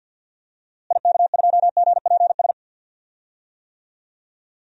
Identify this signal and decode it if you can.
Morse code, heard as a keyed tone, about 50 wpm, 700 Hz. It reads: IK2GPS
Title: DXCC-100 Call Signs - 6 Characters - Repeated 2x - 50wpm